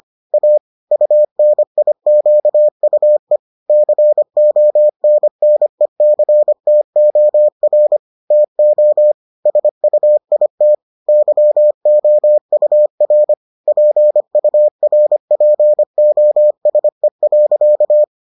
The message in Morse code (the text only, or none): A UNIQUE CONNECTOR TO SUIT YOUR PURPOSE.